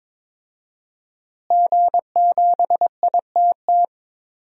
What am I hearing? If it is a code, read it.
Morse code, heard as a keyed tone, about 22 wpm, 700 Hz. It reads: G7ITT
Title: DXCC-100 Call Signs - 5 Characters - Repeated 2x - 22wpm